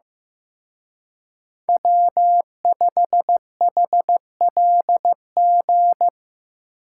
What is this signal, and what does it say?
Morse code, heard as a keyed tone, about 18 wpm, 700 Hz. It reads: W5HLG
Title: Speed Racing - USA 1x3 Call Signs 15wpm